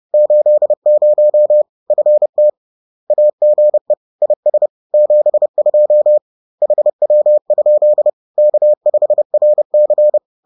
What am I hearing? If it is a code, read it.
Morse code, heard as a keyed tone, about 30 wpm, 600 Hz. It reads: 80 FT AGE IS 72 HW? K5RC
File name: 30wpm